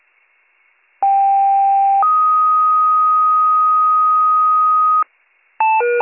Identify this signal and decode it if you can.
Two-tone page, heard as a one-second tone, then a three-second tone, about 790 then 1250 Hz. It is heard from 1.0 to 5.0 s.